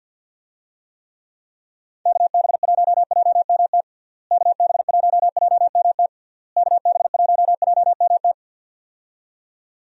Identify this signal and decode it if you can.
Morse code, heard as a keyed tone, about 50 wpm, 700 Hz. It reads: KB1JMT KB1JMT KB1JMT
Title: USA Call Signs - 6 Characters - Repeated 3x - 50wpm